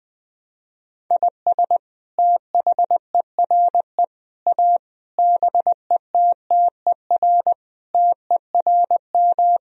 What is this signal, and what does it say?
Morse code, heard as a keyed tone, about 21 wpm, 700 Hz. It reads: IS THERE A BETTER TERM
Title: Sentences from Top 1000 Words - Review Part 4 - 20wpm